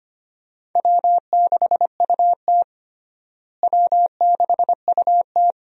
Morse code, heard as a keyed tone, about 25 wpm, 700 Hz. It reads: W6UT W6UT
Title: USA Call Signs - 4 Characters - Repeated 2x - 25wpm